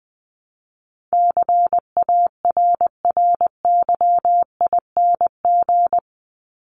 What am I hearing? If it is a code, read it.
Morse code, heard as a keyed tone, about 20 wpm, 700 Hz. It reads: CARRYING
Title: ING Suffix - Top 100 Words with Base Words 20wpm